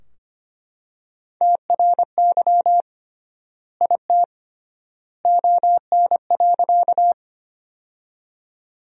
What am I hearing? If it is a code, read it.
Morse code, heard as a keyed tone, about 25 wpm, 700 Hz. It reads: TRY IT ON.
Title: 3x Word-Spacing - Sentences from Top 200 Words - 25wpm